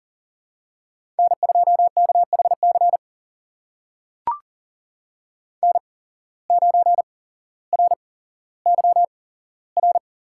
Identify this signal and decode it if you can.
Morse code, heard as a keyed tone, about 40 wpm, spaced as for 17 wpm, 700 Hz. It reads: N2KHC N9RYR
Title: USA Call Signs - 5 Characters - 8x Character Spacing - 40wpm